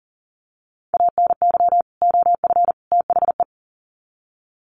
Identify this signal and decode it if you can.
Morse code, heard as a keyed tone, about 40 wpm, 700 Hz. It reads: ANY OF THE
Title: Sets of 3 Words 40wpm